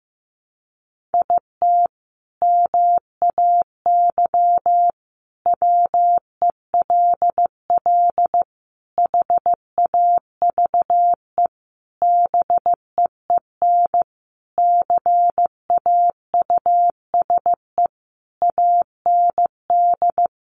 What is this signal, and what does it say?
Morse code, heard as a keyed tone, about 15 wpm, 700 Hz. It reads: IT MAY WELL HAVE BEEN CAUSE AND
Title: Sentences from Top 1000 Words 15wpm